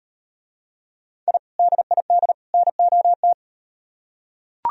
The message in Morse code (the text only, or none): I DID NOT